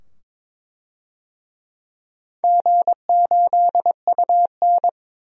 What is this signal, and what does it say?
Morse code, heard as a keyed tone, about 22 wpm, 700 Hz. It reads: G8UN
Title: DXCC-100 Call Signs - 4 Characters - 22wpm